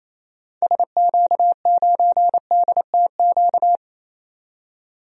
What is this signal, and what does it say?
Morse code, heard as a keyed tone, about 27 wpm, 700 Hz. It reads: SQ9DTQ
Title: DXCC-100 Call Signs - 6 Characters - 28wpm